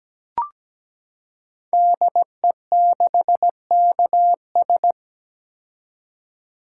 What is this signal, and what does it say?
Morse code, heard as a keyed tone, about 17 wpm, 700 Hz. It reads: DE6KS
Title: Code Groups - 5 Letter:Numbers - 17wpm